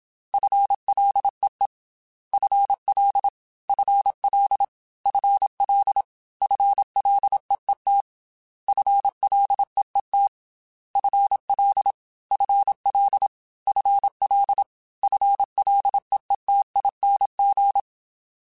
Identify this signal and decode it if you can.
Morse code, heard as a keyed tone, about 27 wpm, 800 Hz. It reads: FLEE FL FL FL FLEET FLEET FL FL FL FLEETING
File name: FL / FL mp3s / FL Prefixes 3x2 @25